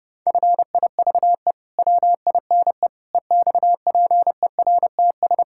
Morse code, heard as a keyed tone, about 32 wpm, 700 Hz. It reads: FIVE WINE EXPERTS
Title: Pangram Challenge v2 - 30wpm